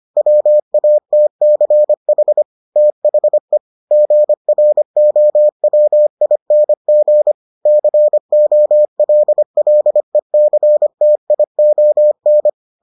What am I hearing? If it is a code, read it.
Morse code, heard as a keyed tone, about 25 wpm, 600 Hz. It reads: WATCH THE GROWING COLLECTION